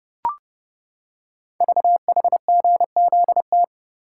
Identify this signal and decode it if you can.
Morse code, heard as a keyed tone, about 28 wpm, 690 Hz. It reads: VHGZT
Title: Code Groups - 5 Letters - 30wpm